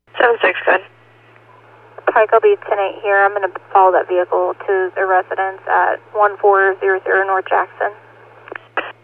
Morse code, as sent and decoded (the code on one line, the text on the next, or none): none